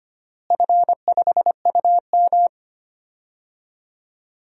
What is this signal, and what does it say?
Morse code, heard as a keyed tone, about 25 wpm, 700 Hz. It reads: F5UM
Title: DXCC-100 Call Signs - 4 Characters - Repeated 3x - 25wpm